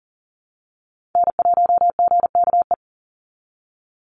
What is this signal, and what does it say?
Morse code, heard as a keyed tone, about 38 wpm, 700 Hz. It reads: N1GKE
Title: Call Signs Repeated 3x - 40wpm